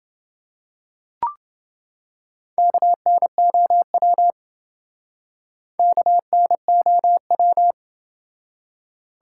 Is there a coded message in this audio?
Yes: Morse code from 2.6 to 7.7 s, heard as a keyed tone, about 27 wpm, 700 Hz. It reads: KNOW KNOW